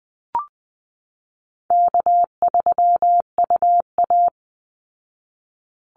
Morse code, heard as a keyed tone, about 19 wpm, 700 Hz. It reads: K3UA
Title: Call Signs Repeated 2x - 20wpm